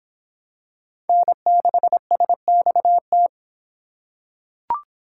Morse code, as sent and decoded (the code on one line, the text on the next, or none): -. -.... ... -..- -
N6SXT